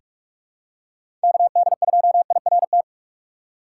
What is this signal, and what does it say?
Morse code, heard as a keyed tone, about 42 wpm, 700 Hz. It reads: KD2IRT
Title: USA Call Signs - 6 Characters - 4x Character Spacing - 45wpm